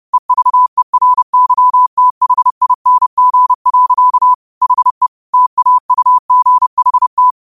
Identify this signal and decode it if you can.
Morse code, heard as a keyed tone, about 30 wpm, 1000 Hz. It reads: EVERYTHING. HE TAUGHT
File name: qotdfiles / 30wpm